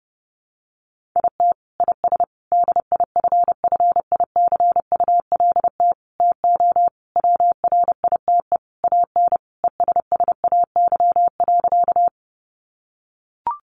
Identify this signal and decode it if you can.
Morse code, heard as a keyed tone, about 30 wpm, 700 Hz. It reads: IT IS DIFFICULT TO WRITE AN ESSAY.
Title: Sentences from Top 1500 Words 30wpm